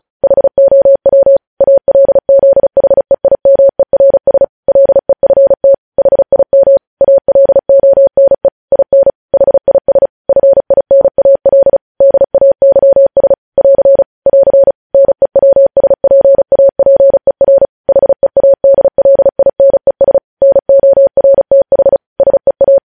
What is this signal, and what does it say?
Morse code, heard as a keyed tone, about 36 wpm, 550 Hz. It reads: HOW ALZHEIMERS LEFT HIM ALONE IN HIS FINAL DAYS <AR> <AR> NEWSPAPER HEADLINES NORTH SEA
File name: podcast / News headlines-1x-35x35